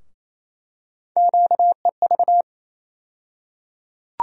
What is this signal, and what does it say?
Morse code, heard as a keyed tone, about 28 wpm, 700 Hz. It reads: QEV
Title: USA Call Sign Postfixes - 3 Letters 28wpm